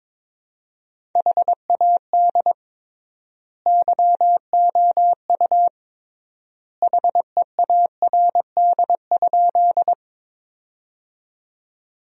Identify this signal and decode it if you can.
Morse code, heard as a keyed tone, about 22 wpm, 700 Hz. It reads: HAD YOU HEARD?
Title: 3x Word-Spacing - Sentences from Top 400 Words - 22wpm